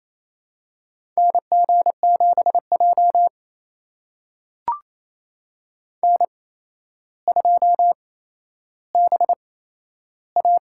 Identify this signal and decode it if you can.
Morse code, heard as a keyed tone, about 27 wpm, spaced as for 12 wpm, 700 Hz. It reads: NG7J N2BA
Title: USA Call Signs - 4 Characters - 8x Character Spacing - 28wpm